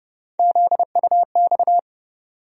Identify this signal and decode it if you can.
Morse code, heard as a keyed tone, about 30 wpm, 700 Hz. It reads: ZUX
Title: USA Call Sign Postfixes - 3 Letters 30wpm